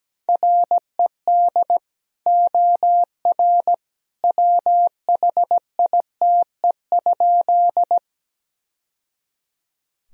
Morse code, heard as a keyed tone, about 17 wpm, 700 Hz. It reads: RED OR WHITE?
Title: Sentences from Top 1000 Words - Review Part 2 - 17wpm